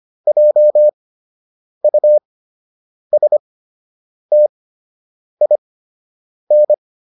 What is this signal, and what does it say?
Morse code, heard as a keyed tone, about 25 wpm, spaced as for 8 wpm, 600 Hz. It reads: JUSTIN